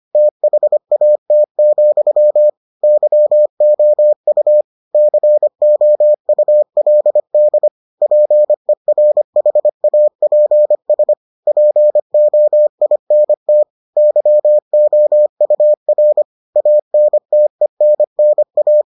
Morse code, heard as a keyed tone, about 25 wpm, 600 Hz. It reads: THAT, YOU COULD PERHAPS POINT YOUR ANTENNA